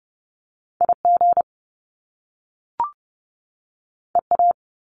Morse code, heard as a keyed tone, about 30 wpm, 700 Hz. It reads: IG EA
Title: Top Two Letter Combinations in Words 30wpm